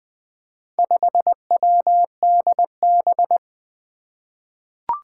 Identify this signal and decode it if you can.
Morse code, heard as a keyed tone, about 20 wpm, 700 Hz. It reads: 5WDB
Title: Four Challenging Letter-Number Combinations 20wpm